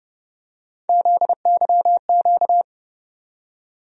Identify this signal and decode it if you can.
Morse code, heard as a keyed tone, about 29 wpm, 700 Hz. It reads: ZYQ
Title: USA Call Sign Postfixes - 3 Letters 30wpm